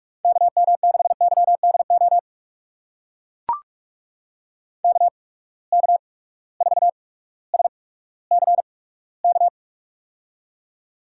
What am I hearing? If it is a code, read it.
Morse code, heard as a keyed tone, about 44 wpm, spaced as for 19 wpm, 700 Hz. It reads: KM6YDO KK4SCK
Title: USA Call Signs - 6 Characters - 8x Character Spacing - 45wpm